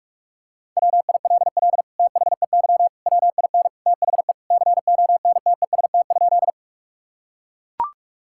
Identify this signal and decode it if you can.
Morse code, heard as a keyed tone, about 41 wpm, 700 Hz. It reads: WILL THEY WIN THE CONTEST?
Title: Sentences from Top 1600 Words 45wpm